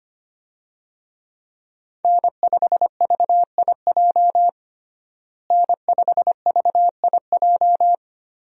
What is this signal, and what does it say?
Morse code, heard as a keyed tone, about 25 wpm, 690 Hz. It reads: N5VIJ N5VIJ
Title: USA Call Signs - 5 Characters - Repeated 2x - 25wpm